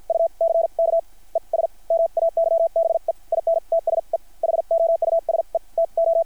Morse code, heard as a keyed tone, about 53 wpm, spaced as for 40 wpm, 690 Hz. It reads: UCKES MAYBE IN TSE HOUSE TO
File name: archived / CRSnet